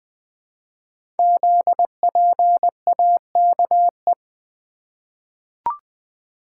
Morse code, heard as a keyed tone, about 20 wpm, 700 Hz. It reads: ZPAKE
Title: Code Groups - 5 Letter:Numbers - 20wpm